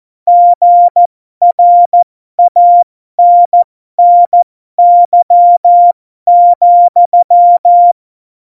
Morse code, heard as a keyed tone, about 14 wpm, spaced as for 12 wpm, 700 Hz. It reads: GRANNY,
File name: Morsekurs / Witze